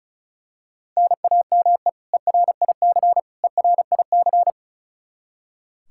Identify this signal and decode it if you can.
Morse code, heard as a keyed tone, about 36 wpm, 700 Hz. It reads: NAME ERIC ERIC
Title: Practice QSO - Exchange 1 - Beginning - 35wpm